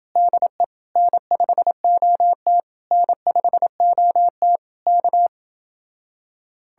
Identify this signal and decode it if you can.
Morse code, heard as a keyed tone, about 31 wpm, 700 Hz. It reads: DE N5OT N5OT K